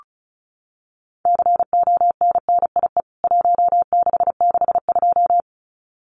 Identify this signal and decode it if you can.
Morse code, heard as a keyed tone, about 34 wpm, 700 Hz. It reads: CONNIE 1662
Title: CWT Exchange 35wpm